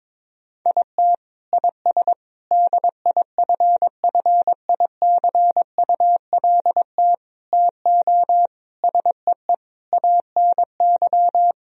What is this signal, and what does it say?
Morse code, heard as a keyed tone, about 22 wpm, 700 Hz. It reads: IT IS DIFFICULT TO SEE ANY
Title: Sentences from Top 1000 Words - Review Part 4 - 22wpm